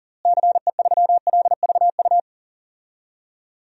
Morse code, heard as a keyed tone, about 40 wpm, 700 Hz. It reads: CE3LVU
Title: DXCC-100 Call Signs - 6 Characters - Repeated 2x - 40wpm